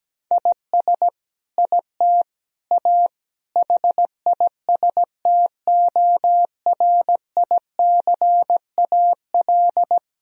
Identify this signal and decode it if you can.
Morse code, heard as a keyed tone, about 17 wpm, 700 Hz. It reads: IS IT A HISTORICAL
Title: Sentences from Top 1000 Words - Review Part 4 - 17wpm